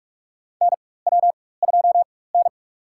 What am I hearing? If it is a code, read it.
Morse code, heard as a keyed tone, about 44 wpm, spaced as for 21 wpm, 690 Hz. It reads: NW2N
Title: USA Call Signs - 4 Characters - 4x Character Spacing - 45wpm